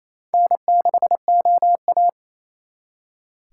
Morse code, heard as a keyed tone, about 35 wpm, 700 Hz. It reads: N6OA
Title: Speed Racing - USA 1x2 Call Signs 28wpm